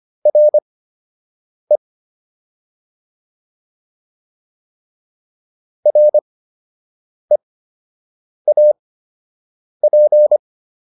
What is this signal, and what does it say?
Morse code, heard as a keyed tone, about 25 wpm, spaced as for 7 wpm, 600 Hz. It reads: RE REAP